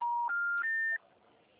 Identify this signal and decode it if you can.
Special information tone.